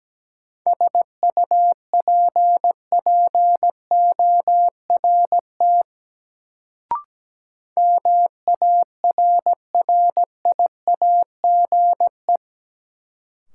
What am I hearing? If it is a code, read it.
Morse code, heard as a keyed tone, about 17 wpm, 700 Hz. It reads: SUPPORT MARRIAGE
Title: Sentences from Top 800 Words - New Word Drill - 17wpm